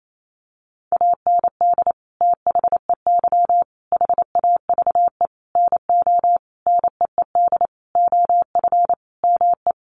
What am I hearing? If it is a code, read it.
Morse code, heard as a keyed tone, about 28 wpm, 700 Hz. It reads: AND THEY HAVE NO NEED OF ME
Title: Sentences from Top 1400 Words 28wpm